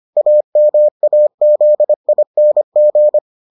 Morse code, heard as a keyed tone, about 25 wpm, 600 Hz. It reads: AMAZING